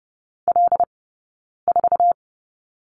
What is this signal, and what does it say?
Morse code, heard as a keyed tone, about 30 wpm, 700 Hz. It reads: L 4
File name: Lesson 19d - Letter w - Two Characters 30wpm